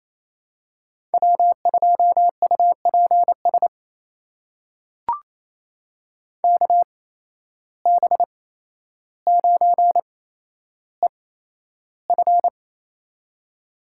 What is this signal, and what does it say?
Morse code, heard as a keyed tone, about 28 wpm, spaced as for 12 wpm, 700 Hz. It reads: W2UPS KB9EF
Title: USA Call Signs - 5 Characters - 8x Character Spacing - 28wpm